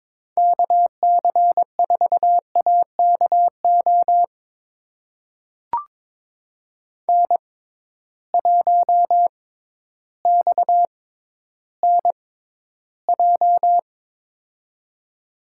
Morse code, heard as a keyed tone, about 22 wpm, spaced as for 12 wpm, 700 Hz. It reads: KC4AKO N1XNJ
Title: USA Call Signs - 6x Character Spacing - 22wpm